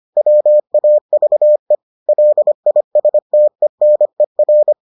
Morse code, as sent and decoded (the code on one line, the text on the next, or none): .-- .- ...- . / .-.. .. ... - . -. .-.
WAVE LISTENER